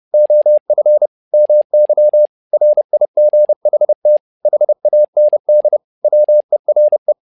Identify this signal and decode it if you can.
Morse code, heard as a keyed tone, about 30 wpm, 600 Hz. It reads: OF MY RIGHT HAND WERE
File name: learning center / tower trouble / tower trouble 30wpm